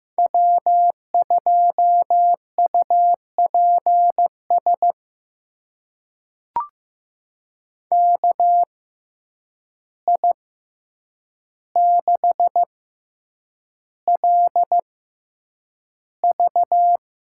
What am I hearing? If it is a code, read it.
Morse code, heard as a keyed tone, about 15 wpm, spaced as for 8 wpm, 700 Hz. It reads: W2UPS KI6LV
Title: USA Call Signs - 6x Character Spacing - 15wpm